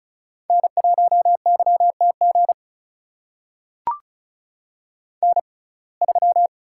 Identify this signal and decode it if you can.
Morse code, heard as a keyed tone, about 34 wpm, spaced as for 25 wpm, 700 Hz. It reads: N1YTG N3
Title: USA Call Signs - 5 Characters - 6x Character Spacing - 35wpm